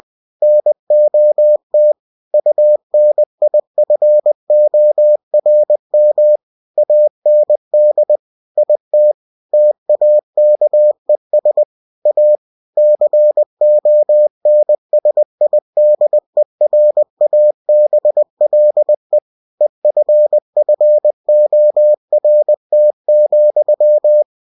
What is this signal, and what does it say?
Morse code, heard as a keyed tone, about 20 wpm, 600 Hz. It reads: NOT UNIFORM AND IT TAKES A CONSIDERABLE EFFORT,